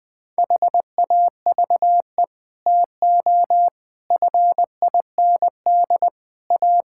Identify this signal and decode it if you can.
Morse code, heard as a keyed tone, about 21 wpm, 700 Hz. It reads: HAVE TO FIND A
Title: Sets of 6 Words - 20wpm